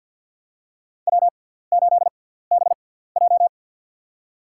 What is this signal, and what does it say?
Morse code, heard as a keyed tone, about 53 wpm, spaced as for 17 wpm, 700 Hz. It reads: W8BJ
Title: USA Call Signs - 6x Character Spacing - 50wpm